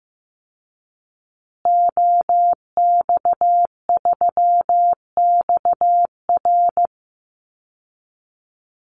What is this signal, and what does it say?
Morse code, heard as a keyed tone, about 15 wpm, 700 Hz. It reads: OX3XR
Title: Call Signs Repeated 3x - 15wpm